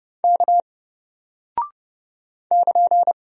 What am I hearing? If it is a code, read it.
Morse code, heard as a keyed tone, about 30 wpm, 700 Hz. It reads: K <KN>
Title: Single Character (Letter:Number:Punctuation:Prosign) Alt 30wpm